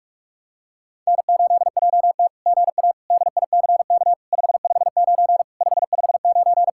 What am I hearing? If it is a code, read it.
Morse code, heard as a keyed tone, about 44 wpm, 700 Hz. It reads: N8JT GA DICK 559 559
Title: SOTA QSO Practice 45wpm